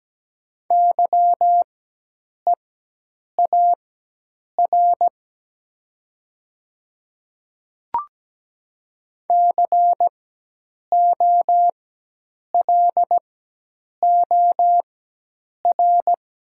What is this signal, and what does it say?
Morse code, heard as a keyed tone, about 17 wpm, spaced as for 8 wpm, 700 Hz. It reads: YEAR COLOR
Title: 4x Character Spacing - Top 300 Words - 17wpm